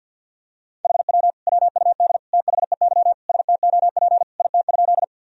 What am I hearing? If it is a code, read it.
Morse code, heard as a keyed tone, about 52 wpm, 700 Hz. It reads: HW WUD THEY STOP IT?
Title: Sentences from Top 300 Words and Abbreviations 50wpm